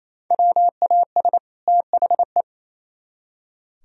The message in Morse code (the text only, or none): WAS THE